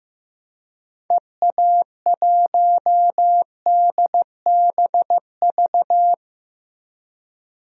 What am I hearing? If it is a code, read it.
Morse code, heard as a keyed tone, about 15 wpm, 700 Hz. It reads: EA1DBV
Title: DXCC-100 Call Signs - 6 Characters - Repeated 3x - 15wpm